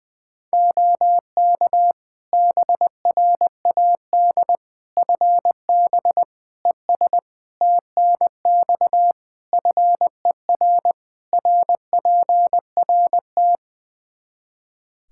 Morse code, heard as a keyed tone, about 20 wpm, 700 Hz. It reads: OK BRAD FB ES TNX FER RPRT
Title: K1USN SST QSO Practice 20wpm